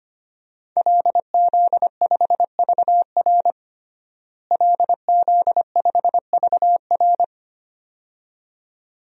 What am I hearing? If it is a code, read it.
Morse code, heard as a keyed tone, about 25 wpm, 700 Hz. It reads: LZ5VR LZ5VR
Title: DXCC-100 Call Signs - 5 Characters - Repeated 2x - 25wpm